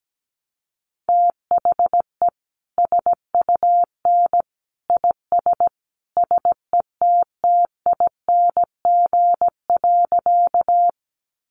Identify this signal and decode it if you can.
Morse code, heard as a keyed tone, about 17 wpm, 700 Hz. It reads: THE SUN IS SETTING.
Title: Sentences from Top 1300 Words 17wpm